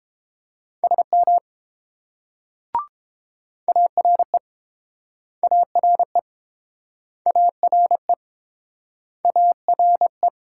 Morse code, heard as a keyed tone, about 30 wpm, 700 Hz. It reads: SM ARE ARE ARE ARE